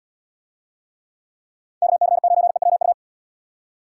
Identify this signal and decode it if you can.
Morse code, heard as a keyed tone, about 75 wpm, 700 Hz. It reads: KC9EWU